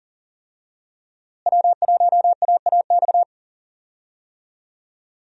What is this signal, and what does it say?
Morse code, heard as a keyed tone, about 37 wpm, 700 Hz. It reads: W1AAX
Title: USA Call Signs - 5 Characters - Repeated 2x - 40wpm